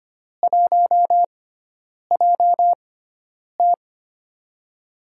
Morse code, heard as a keyed tone, about 25 wpm, spaced as for 9 wpm, 700 Hz. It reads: 1JT
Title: USA Call Signs - 4 Characters - 6x Character Spacing - 25wpm